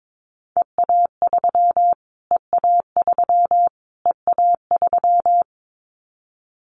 Morse code, heard as a keyed tone, about 22 wpm, 700 Hz. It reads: EA3 EA3 EA3
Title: Callsign Prefixes for Top 25 DXCC - Repeated 3x - 22wpm